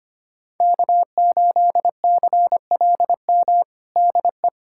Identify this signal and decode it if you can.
Morse code, heard as a keyed tone, about 25 wpm, 700 Hz. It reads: K8CLM DE
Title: USA Callsign DE Callsign - 5 Characters - 25wpm